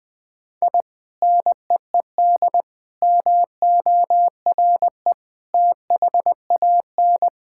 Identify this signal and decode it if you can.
Morse code, heard as a keyed tone, about 20 wpm, 700 Hz. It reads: I NEED MORE THAN
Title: Sentences with 2-Word Phrases - 20wpm